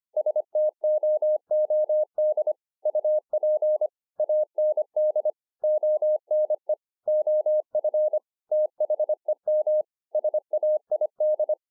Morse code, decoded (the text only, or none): STOOD UP AND ONE OF THEM SAID